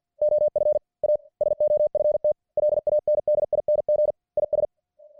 Morse code, heard as a keyed tone, about 50 wpm, 600 Hz. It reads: OF A SOFT LANDING IS